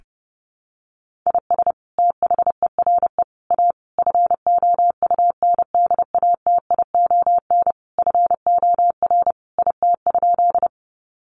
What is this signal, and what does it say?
Morse code, heard as a keyed tone, about 30 wpm, 700 Hz. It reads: IS THERE A FOUNDATION FOR IT?
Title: Sentences from Top 1200 Words 30wpm